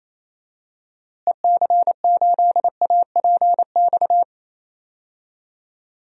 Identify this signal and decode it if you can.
Morse code, heard as a keyed tone, about 28 wpm, 700 Hz. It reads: EC8APX
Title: DXCC-100 Call Signs - 6 Characters - Repeated 3x - 28wpm